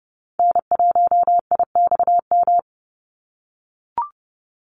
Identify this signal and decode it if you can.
Morse code, heard as a keyed tone, about 30 wpm, 700 Hz. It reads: N1IXM
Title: USA Call Signs 30wpm